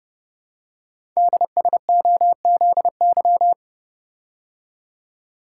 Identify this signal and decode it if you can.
Morse code, heard as a keyed tone, about 30 wpm, 690 Hz. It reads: DSOZY
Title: Code Groups - 5 Letters - 30wpm